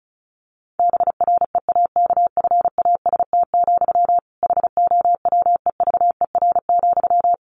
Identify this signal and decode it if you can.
Morse code, heard as a keyed tone, about 34 wpm, 700 Hz. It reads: BREAKFAST, HOWEVER,
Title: Sentences from Top 1500 Words 35wpm